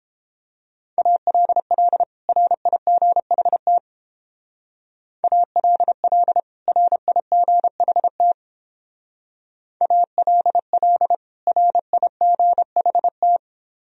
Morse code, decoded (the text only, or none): ALL RIGHT ALL RIGHT ALL RIGHT